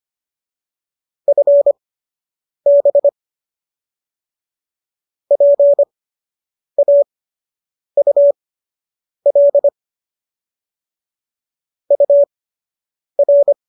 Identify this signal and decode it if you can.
Morse code, heard as a keyed tone, about 25 wpm, spaced as for 8 wpm, 570 Hz. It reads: FB PAUL UR
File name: qso006 08